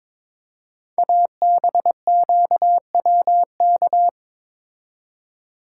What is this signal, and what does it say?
Morse code, heard as a keyed tone, about 22 wpm, 700 Hz. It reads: ABQWK
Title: Code Groups - 5 Letters - 22wpm